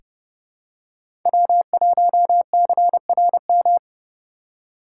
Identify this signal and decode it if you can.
Morse code, heard as a keyed tone, about 30 wpm, 700 Hz. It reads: W1CRM
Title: USA Call Signs - 3x Character Spacing - 30wpm